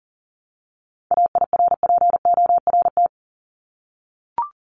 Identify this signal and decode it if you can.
Morse code, heard as a keyed tone, about 40 wpm, 700 Hz. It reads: AIRPORT
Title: Top 1751-2000 Words 40wpm